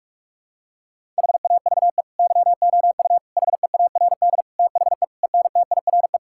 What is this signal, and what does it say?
Morse code, heard as a keyed tone, about 41 wpm, 700 Hz. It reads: HAVE YOU HEARD THE ENTIRE
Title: Sentences from Top 1000 Words - Review Part 5 - 45wpm